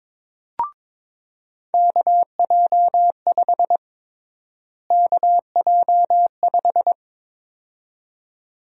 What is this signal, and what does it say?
Morse code, heard as a keyed tone, about 22 wpm, 700 Hz. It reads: KJ5 KJ5
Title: USA Call Sign Prefixes - Repeated 2x - 22wpm